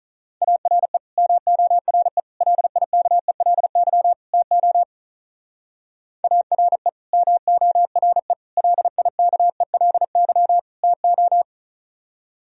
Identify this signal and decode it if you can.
Morse code, heard as a keyed tone, about 44 wpm, 700 Hz. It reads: ARE MORE LIKELY TO ARE MORE LIKELY TO
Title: Speed-Racing - Sets of 4 Words 35wpm